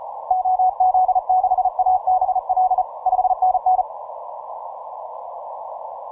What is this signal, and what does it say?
Morse code, heard as a keyed tone, about 33 wpm, 740 Hz. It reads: OZ6ABL 5NN
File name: simple-file-list / C92RU 12MCW